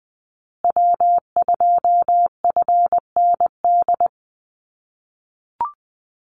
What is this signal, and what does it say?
Morse code, heard as a keyed tone, about 20 wpm, 700 Hz. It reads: W2FND
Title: USA Call Signs Repeated 2x 20wpm